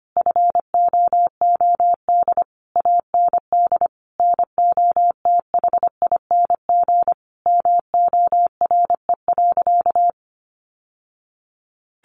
Morse code, heard as a keyed tone, about 25 wpm, 700 Hz. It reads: FOOD AND NOTHING MORE.
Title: Sentences from Top 800 Words 25wpm